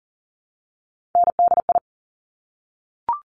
Morse code, heard as a keyed tone, about 41 wpm, 690 Hz. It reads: NDI
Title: Top Three Letter Combinations in Words 40wpm